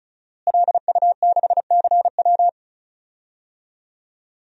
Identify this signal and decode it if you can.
Morse code, heard as a keyed tone, about 34 wpm, 700 Hz. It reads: LU6CW
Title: DXCC-100 Call Signs - 5 Characters - Repeated 2x - 35wpm